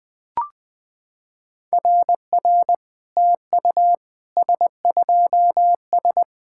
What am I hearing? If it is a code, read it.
Morse code, heard as a keyed tone, about 20 wpm, 700 Hz. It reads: RR TU S2S
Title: SOTA QSO Practice 20wpm